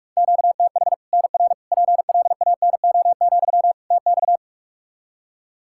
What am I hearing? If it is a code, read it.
Morse code, heard as a keyed tone, about 46 wpm, 700 Hz. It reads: QTH NR PLANO, TX
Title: Practice QSO - Exchange 1 - Beginning - 45wpm